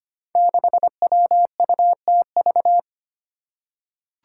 Morse code, heard as a keyed tone, about 25 wpm, 700 Hz. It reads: 6WUTV
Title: Five Challenging Letter-Number Combinations 25wpm